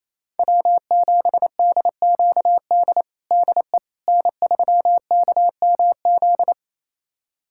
Morse code, heard as a keyed tone, about 28 wpm, 700 Hz. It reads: W7DQD DE N3KMZ
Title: USA Callsign DE Callsign - 5 Characters - 28wpm